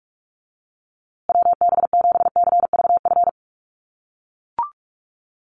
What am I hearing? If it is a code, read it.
Morse code, heard as a keyed tone, about 44 wpm, 700 Hz. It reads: WB7CVF
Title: USA Call Signs Repeated 2x 45wpm